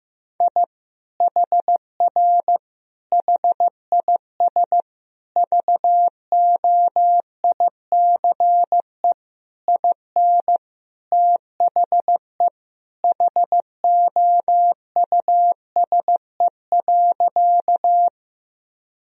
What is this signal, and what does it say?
Morse code, heard as a keyed tone, about 16 wpm, 700 Hz. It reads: I HR HIS VOICE IN THE HOUSE.
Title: Sentences from Top 500 Words and Abbreviations 15wpm